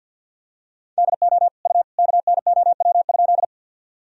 Morse code, heard as a keyed tone, about 50 wpm, 700 Hz. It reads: DO U KNOW?
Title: Sentences from Top 100 Words and Abbreviations 50wpm